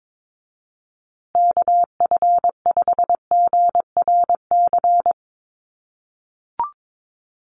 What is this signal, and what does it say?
Morse code, heard as a keyed tone, about 22 wpm, 700 Hz. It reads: KF5GRC
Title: USA Call Signs Repeated 2x 22wpm